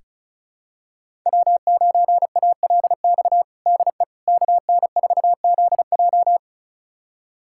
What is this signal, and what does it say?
Morse code, heard as a keyed tone, about 36 wpm, 700 Hz. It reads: W9ALX DE KN4ZJ
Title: USA Callsign DE Callsign - 5 Characters - 35wpm